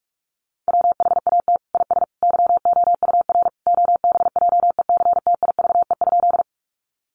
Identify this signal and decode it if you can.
Morse code, heard as a keyed tone, about 44 wpm, 700 Hz. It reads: WHAT IS YOUR OBJECTIVE?
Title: Sentences from Top 1200 Words 45wpm